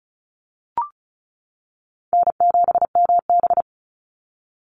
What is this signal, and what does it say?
Morse code, heard as a keyed tone, about 34 wpm, 700 Hz. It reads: N7MB
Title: Call Signs - 35wpm